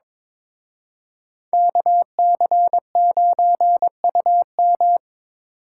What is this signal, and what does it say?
Morse code, heard as a keyed tone, about 26 wpm, 700 Hz. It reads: KC9UM
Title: Speed Racing - USA 2x2 Call Signs 22wpm